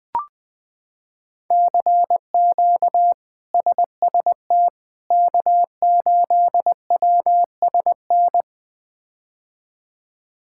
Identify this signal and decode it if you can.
Morse code, heard as a keyed tone, about 21 wpm, 700 Hz. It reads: CQ SST K8WSN
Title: K1USN SST QSO Practice - 20wpm